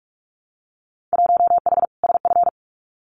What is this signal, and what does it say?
Morse code, heard as a keyed tone, about 46 wpm, 700 Hz. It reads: 1H SF
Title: Winter Field Day Exchanges 45wpm